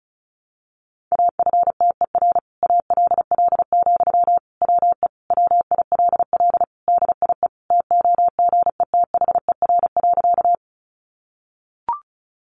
Morse code, heard as a keyed tone, about 34 wpm, 700 Hz. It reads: AFTER ALL, WE WILL DIE TOGETHER.
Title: Sentences from Top 1300 Words 35wpm